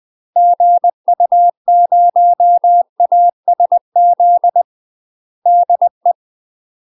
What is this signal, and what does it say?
Morse code, heard as a keyed tone, about 20 wpm, 700 Hz. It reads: GU0ASZ DE
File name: UK MASTER QSOs TRAINING / 20WPM / UK MASTER QSOs 0044